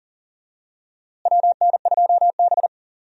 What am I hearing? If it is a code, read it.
Morse code, heard as a keyed tone, about 40 wpm, 700 Hz. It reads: WN2B
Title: USA Call Signs - 4 Characters - Repeated 3x - 40wpm